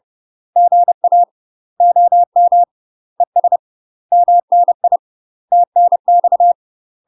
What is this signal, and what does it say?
Morse code, heard as a keyed tone, about 30 wpm, 700 Hz. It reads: GA OM ES MNI TNX
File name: UK MASTER QSOs TRAINING / 30WPM / UK MASTER QSOs 0048